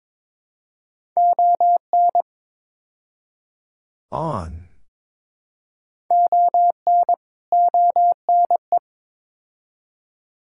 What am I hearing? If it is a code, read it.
Morse code, heard as a keyed tone, about 22 wpm, 700 Hz. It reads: ON ON ONE
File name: practice / 22wpm / Build A Phrase - 5-Word Phrases - 22wpm